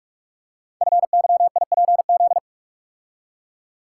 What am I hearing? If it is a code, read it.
Morse code, heard as a keyed tone, about 44 wpm, 700 Hz. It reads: FYIPZ
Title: Code Groups - 5 Letter:Numbers - 45wpm